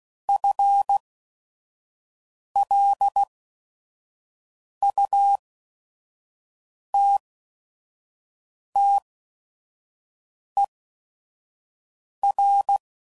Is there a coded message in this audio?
Yes: Morse code from 0.3 to 12.8 s, heard as a keyed tone, about 16 wpm, spaced as for 5 wpm, 800 Hz. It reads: FLUTTER